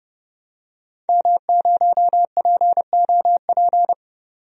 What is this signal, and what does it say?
Morse code, heard as a keyed tone, about 31 wpm, 700 Hz. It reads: M0POP
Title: DXCC-100 Call Signs - 5 Characters - 30wpm